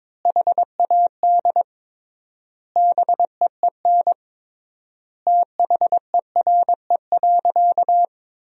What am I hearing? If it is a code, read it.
Morse code, heard as a keyed tone, about 22 wpm, 700 Hz. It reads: HAD BEEN THERE.
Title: 3x Word-Spacing - Sentences from Top 300 Words - 22wpm